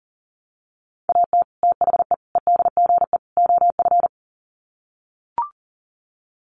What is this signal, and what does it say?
Morse code, heard as a keyed tone, about 40 wpm, 700 Hz. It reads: AT THE EDGE OF
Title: Sets of 4 Words 40wpm